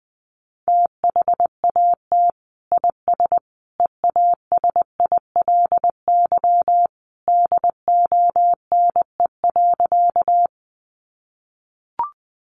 Morse code, heard as a keyed tone, about 20 wpm, 700 Hz. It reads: THAT IS EASILY DONE.
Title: Sentences from Top 900 Words 20wpm